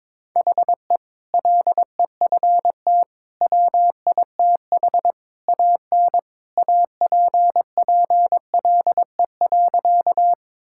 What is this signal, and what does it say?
Morse code, heard as a keyed tone, about 22 wpm, 700 Hz. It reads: HE LEFT WITH AN APPLE.
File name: practice / 22wpm / Sentences with 2-Word Phrases - 22wpm